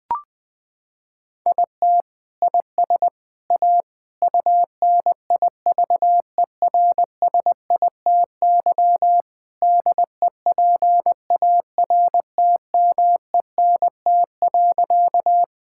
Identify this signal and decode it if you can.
Morse code, heard as a keyed tone, about 20 wpm, 700 Hz. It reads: IT IS A UNIVERSITY DEPARTMENT.
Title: Sentences from Top 1000 Words - Review Part 5 - 20wpm